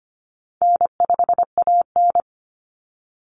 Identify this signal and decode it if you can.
Morse code, heard as a keyed tone, about 25 wpm, 700 Hz. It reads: N5AN
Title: Call Signs - 25wpm